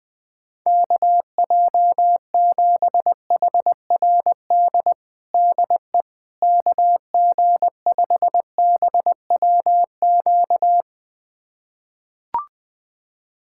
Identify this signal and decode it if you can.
Morse code, heard as a keyed tone, about 20 wpm, 700 Hz. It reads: KJ7HRD DE KG5BWQ
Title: USA Callsign DE Callsign 20wpm